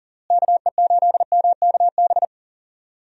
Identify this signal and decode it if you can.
Morse code, heard as a keyed tone, about 40 wpm, 700 Hz. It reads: KE8MKB
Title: USA Call Signs - 6 Characters - Repeated 2x - 40wpm